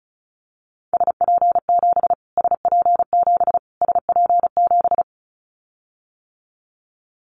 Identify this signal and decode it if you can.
Morse code, heard as a keyed tone, about 34 wpm, 700 Hz. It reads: SP7 SP7 SP7
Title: Callsign Prefixes for Top 25 DXCC - Repeated 3x - 35wpm